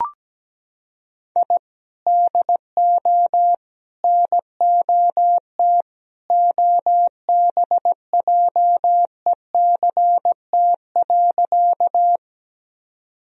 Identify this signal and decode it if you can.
Morse code, heard as a keyed tone, about 17 wpm, 700 Hz. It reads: I DO NOT OBJECT.
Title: Sentences from Top 1000 Words - Review Part 3 - 17wpm